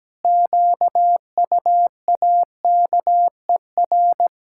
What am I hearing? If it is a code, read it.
Morse code, heard as a keyed tone, about 17 wpm, 700 Hz. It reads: QUAKER
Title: Pangram Challenge v2 - 17wpm